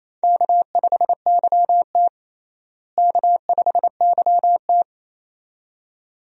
Morse code, heard as a keyed tone, about 28 wpm, 700 Hz. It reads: K5YT K5YT
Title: USA Call Signs - 4 Characters - Repeated 2x - 28wpm